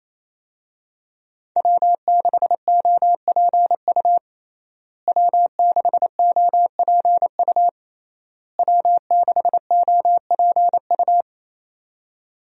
Morse code, heard as a keyed tone, about 28 wpm, 700 Hz. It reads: W6OPU W6OPU W6OPU
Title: USA Call Signs - 5 Characters - Repeated 3x - 28wpm